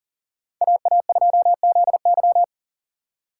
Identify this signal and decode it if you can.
Morse code, heard as a keyed tone, about 38 wpm, 700 Hz. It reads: AA2ZY
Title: USA Call Signs - 3x Character Spacing - 40wpm